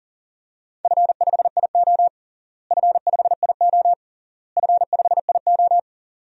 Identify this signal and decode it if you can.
Morse code, heard as a keyed tone, about 37 wpm, 700 Hz. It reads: F5IO F5IO F5IO
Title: DXCC-100 Call Signs - 4 Characters - Repeated 3x - 40wpm